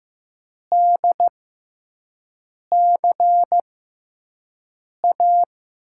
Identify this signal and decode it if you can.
Morse code, heard as a keyed tone, about 15 wpm, spaced as for 5 wpm, 700 Hz. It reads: DCA
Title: USA Call Signs - 6 Characters - 6x Character Spacing - 15wpm